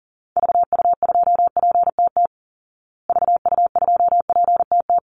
Morse code, heard as a keyed tone, about 40 wpm, 700 Hz. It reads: VU2PTT VU2PTT
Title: Call Signs Repeated 3x - 40wpm